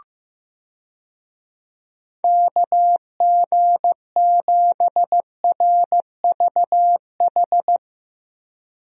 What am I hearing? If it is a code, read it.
Morse code, heard as a keyed tone, about 15 wpm, 700 Hz. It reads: KG7RVH
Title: USA Call Signs - 6 Characters - 15wpm